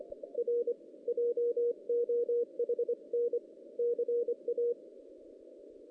Morse code, heard as a keyed tone, about 23 wpm, 440 Hz. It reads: R JOHN CA
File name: 2016 JAN CW NAQP